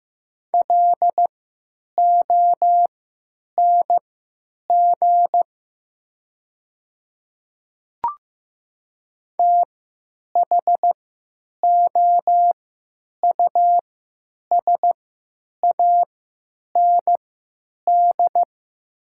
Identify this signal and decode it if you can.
Morse code, heard as a keyed tone, about 15 wpm, spaced as for 9 wpm, 700 Hz. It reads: LONG THOUSAND
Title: 3x Character Spacing - Top 500 Words - 15wpm